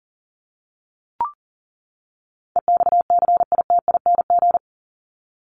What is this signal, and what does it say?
Morse code, heard as a keyed tone, about 40 wpm, 700 Hz. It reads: EXCITING
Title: Top 1751-2000 Words 40wpm